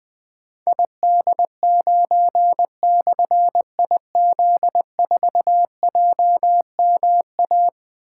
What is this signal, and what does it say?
Morse code, heard as a keyed tone, about 20 wpm, 700 Hz. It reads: ID9/IZ4JMA
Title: Difficult Call Signs 20wpm